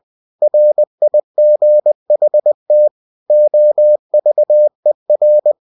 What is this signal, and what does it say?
Morse code, heard as a keyed tone, about 20 wpm, 600 Hz. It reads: RIGHT OVER
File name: cw-books / winnie-the-pooh / winnie-20wpm-page-0068